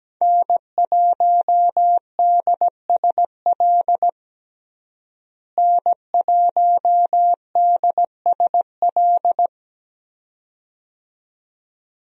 Morse code, heard as a keyed tone, about 17 wpm, 700 Hz. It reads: N1DSL N1DSL
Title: USA Call Signs - 5 Characters - Repeated 2x - 17wpm